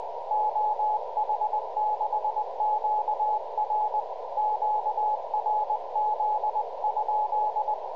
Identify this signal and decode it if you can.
Morse code, heard as a keyed tone, about 20 wpm, 850 Hz, with radio noise. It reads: OH6QR QRZ?